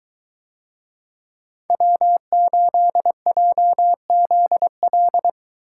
Morse code, heard as a keyed tone, about 22 wpm, 700 Hz. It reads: W8JZL